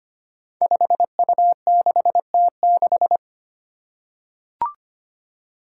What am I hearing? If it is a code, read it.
Morse code, heard as a keyed tone, about 25 wpm, 700 Hz. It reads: 5U6T6
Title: Five Challenging Letter-Number Combinations 25wpm